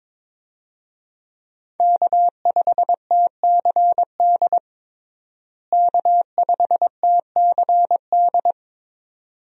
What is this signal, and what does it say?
Morse code, heard as a keyed tone, about 22 wpm, 700 Hz. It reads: K5TCD K5TCD
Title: USA Call Signs - 5 Characters - Repeated 2x - 22wpm